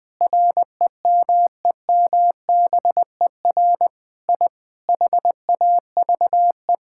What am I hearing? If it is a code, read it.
Morse code, heard as a keyed tone, about 20 wpm, 700 Hz. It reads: REMEMBER I HAVE
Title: Sentences with 8-Word Phrases - Spoken Beginning - 20wpm